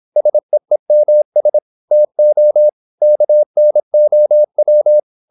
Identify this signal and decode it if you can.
Morse code, heard as a keyed tone, about 26 wpm, 600 Hz. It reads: SEEMS TO KNOW
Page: Never 26wpm | CW med Gnister